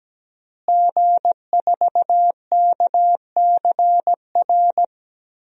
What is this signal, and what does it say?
Morse code, heard as a keyed tone, about 17 wpm, 700 Hz. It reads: G4KCR
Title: DXCC-100 Call Signs - 5 Characters - Repeated 2x - 17wpm